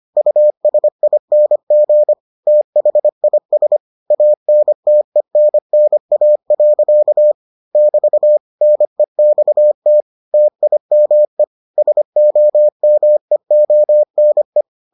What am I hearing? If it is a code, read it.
Morse code, heard as a keyed tone, about 25 wpm, 600 Hz. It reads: USING THIS ANTENNA. <BT> NEXT TIME SOMEONE